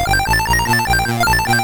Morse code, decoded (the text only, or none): none